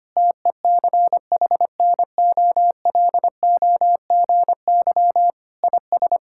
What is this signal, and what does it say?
Morse code, heard as a keyed tone, about 25 wpm, 700 Hz. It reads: TECHNOLOGY IS
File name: practice / 25wpm / Sentences with 2-Word Phrases - 25wpm